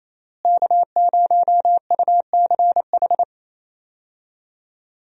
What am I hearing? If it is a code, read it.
Morse code, heard as a keyed tone, about 28 wpm, 700 Hz. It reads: K0UCH
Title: USA Call Signs - 5 Characters - Repeated 3x - 28wpm